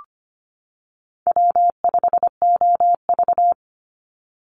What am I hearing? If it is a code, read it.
Morse code, heard as a keyed tone, about 25 wpm, 700 Hz. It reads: W5OV
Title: Call Signs - 25wpm